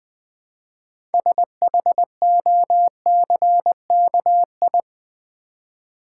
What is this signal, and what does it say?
Morse code, heard as a keyed tone, about 20 wpm, 700 Hz. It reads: SHOCKI
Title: Build A Word - 2 Letters At A Time - 8-Letter Words - 20wpm